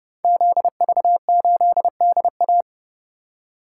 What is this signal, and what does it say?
Morse code, heard as a keyed tone, about 30 wpm, 700 Hz. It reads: ZV8DA
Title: Code Groups - 5 Letter:Numbers - 30wpm